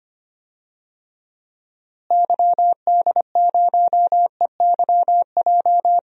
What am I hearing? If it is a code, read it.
Morse code, heard as a keyed tone, about 25 wpm, 700 Hz. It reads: YD0EYJ
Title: DXCC-100 Call Signs - 6 Characters - Repeated 2x - 25wpm